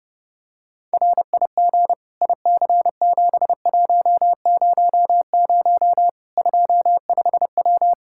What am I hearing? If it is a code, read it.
Morse code, heard as a keyed tone, about 30 wpm, 700 Hz. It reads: RIG IC7100 25W
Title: K1USN SST QSO Practice 30wpm